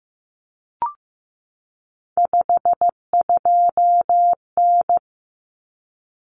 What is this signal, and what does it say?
Morse code, heard as a keyed tone, about 15 wpm, 700 Hz. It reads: 52N
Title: Common Cut Numbers - 3-Digit 15wpm